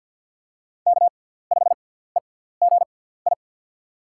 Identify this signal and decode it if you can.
Morse code, heard as a keyed tone, about 52 wpm, spaced as for 17 wpm, 700 Hz. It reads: K5EGI
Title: USA Call Signs - 5 Characters - 6x Character Spacing - 50wpm